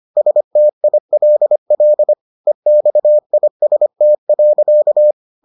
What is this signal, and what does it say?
Morse code, heard as a keyed tone, about 25 wpm, 600 Hz. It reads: STILL EXIST.